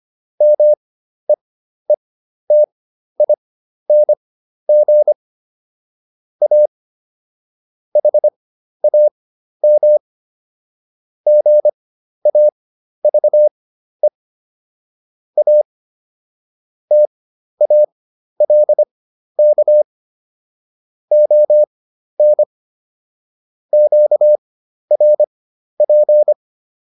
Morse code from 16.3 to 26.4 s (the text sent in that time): TALK ON QRP